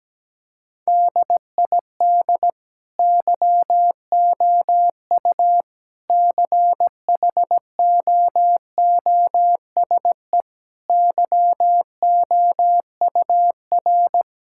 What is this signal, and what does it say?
Morse code, heard as a keyed tone, about 17 wpm, 700 Hz. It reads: DID YOU CHOOSE YOUR
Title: Sentences from Top 1000 Words - Review Part 6 - 17wpm